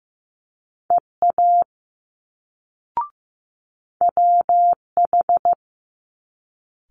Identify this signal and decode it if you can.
Morse code, heard as a keyed tone, about 15 wpm, 700 Hz. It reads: EA WH
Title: Multi-letter Phonemes Quick Fire 15wpm